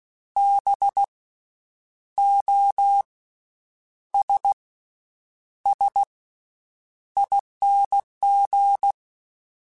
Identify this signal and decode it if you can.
Morse code, heard as a keyed tone, about 16 wpm, spaced as for 8 wpm, 800 Hz. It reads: BOSSING
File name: Special INGs / Special INGs-mp3s / VerySlowToFaster Ing @15